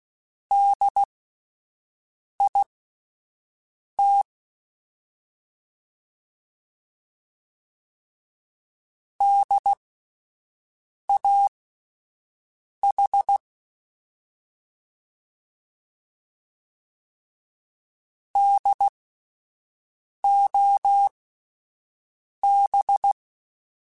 Morse code, heard as a keyed tone, about 16 wpm, spaced as for 5 wpm, 800 Hz. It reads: DIT DAH DOB